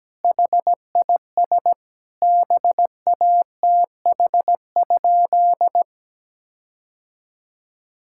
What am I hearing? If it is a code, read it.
Morse code, heard as a keyed tone, about 17 wpm, 700 Hz. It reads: HIS BATH?
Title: Sentences from Top 1600 Words 17wpm